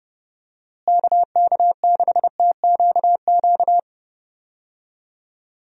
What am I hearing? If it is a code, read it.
Morse code, heard as a keyed tone, about 30 wpm, 700 Hz. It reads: KK6TQQ
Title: USA Call Signs - 6 Characters - Repeated 3x - 30wpm